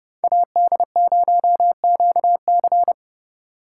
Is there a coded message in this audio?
Yes: Morse code from 0.2 to 2.9 s, heard as a keyed tone, about 30 wpm, 700 Hz. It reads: AD0QC